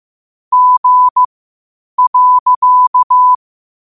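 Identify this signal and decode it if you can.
Morse code, heard as a keyed tone, about 15 wpm, spaced as for 8 wpm, 1000 Hz. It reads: G.
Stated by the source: Morse code quotes at 8 words per minute from QOTD at 8 WPM.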